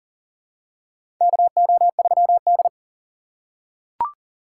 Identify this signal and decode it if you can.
Morse code, heard as a keyed tone, about 40 wpm, 700 Hz. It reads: KO3D
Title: USA Call Signs - 3x Character Spacing - 40wpm